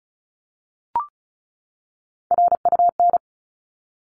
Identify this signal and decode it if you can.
Morse code, heard as a keyed tone, about 34 wpm, 700 Hz. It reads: RUN
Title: Top 300 Words 35wpm